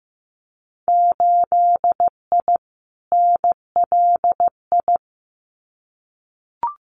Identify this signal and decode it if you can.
Morse code, heard as a keyed tone, about 15 wpm, 700 Hz. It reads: 8I NLI
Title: Winter Field Day Exchanges Encore 15wpm